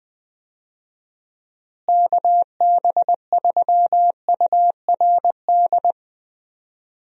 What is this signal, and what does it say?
Morse code, heard as a keyed tone, about 20 wpm, 700 Hz. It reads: KB3URD
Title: USA Call Signs - 6 Characters - Repeated 3x - 20wpm